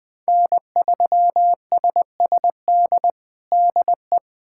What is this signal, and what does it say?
Morse code, heard as a keyed tone, about 20 wpm, 700 Hz. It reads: N3SSD DE
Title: USA Callsign DE Callsign - 5 Characters - 20wpm